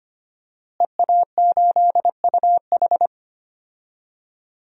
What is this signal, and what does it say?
Morse code, heard as a keyed tone, about 25 wpm, 700 Hz. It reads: EA8UH
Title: DXCC-100 Call Signs - 5 Characters - Repeated 2x - 25wpm